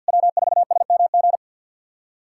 Morse code, heard as a keyed tone, about 48 wpm, 700 Hz. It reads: W4SMG
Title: SOTA QSO Practice 50wpm